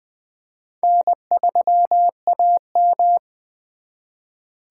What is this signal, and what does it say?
Morse code, heard as a keyed tone, about 24 wpm, 700 Hz. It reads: N3AM
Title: Speed Racing - USA 1x2 Call Signs 20wpm